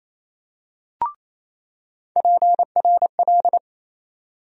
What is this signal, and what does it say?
Morse code, heard as a keyed tone, about 28 wpm, 700 Hz. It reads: PRL
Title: USA Call Sign Postfixes - 3 Letters 28wpm